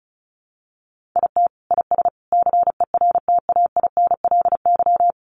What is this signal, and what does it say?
Morse code, heard as a keyed tone, about 34 wpm, 700 Hz. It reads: IT IS CERTAINLY
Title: Sentences from Top 1100 Words 35wpm